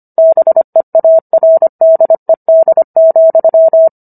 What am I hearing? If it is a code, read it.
Morse code, heard as a keyed tone, about 25 wpm, 650 Hz. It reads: BEARDED,